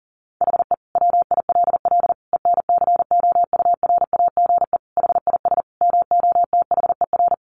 Morse code, heard as a keyed tone, about 40 wpm, 700 Hz. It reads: HE WILL ENCOURAGE HIS MOTHER
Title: Sentences from Top 1400 Words 40wpm